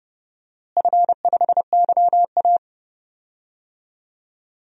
Morse code, heard as a keyed tone, about 32 wpm, 700 Hz. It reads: F5YA
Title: DXCC-100 Call Signs - 4 Characters - Repeated 2x - 30wpm